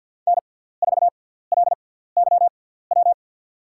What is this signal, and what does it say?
Morse code, heard as a keyed tone, about 53 wpm, spaced as for 17 wpm, 700 Hz. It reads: N4LYW
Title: USA Call Signs - 6x Character Spacing - 50wpm